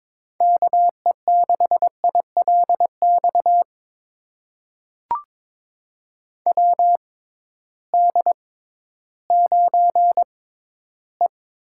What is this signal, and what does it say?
Morse code, heard as a keyed tone, about 21 wpm, spaced as for 13 wpm, 700 Hz. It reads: KE6ILX WD9E
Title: USA Call Signs - 6 Characters - 6x Character Spacing - 22wpm